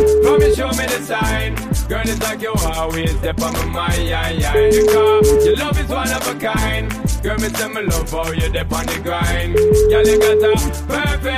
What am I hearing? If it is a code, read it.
Ringback tone (Europe, 425 Hz), with noise.